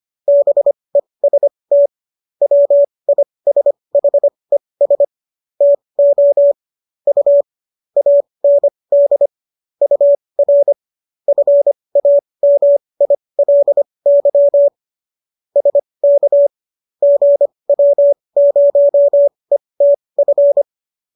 Morse code, heard as a keyed tone, about 25 wpm, spaced as for 20 wpm, 570 Hz. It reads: BEST WISHES TO U AND UR FAMILY SK GW0ETF